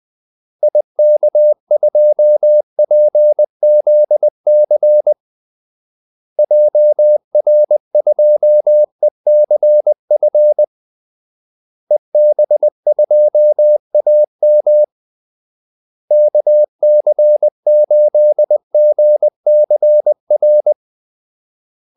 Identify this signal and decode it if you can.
Morse code, heard as a keyed tone, about 20 wpm, 600 Hz. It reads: IK2PZC JR2ECF EB2AM KC8GCR
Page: Kaldesignaler 20 wpm